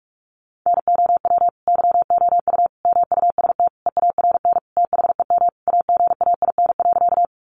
Morse code, heard as a keyed tone, about 44 wpm, 700 Hz. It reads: NOW YOU MUST EARN THEM AGAIN.
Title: Sentences from Top 1500 Words 45wpm